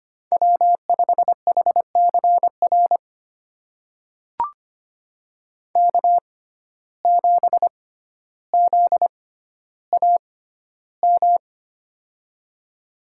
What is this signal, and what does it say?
Morse code, heard as a keyed tone, about 25 wpm, spaced as for 13 wpm, 700 Hz. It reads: W5HCR K7ZAM
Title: USA Call Signs - 5 Characters - 6x Character Spacing - 25wpm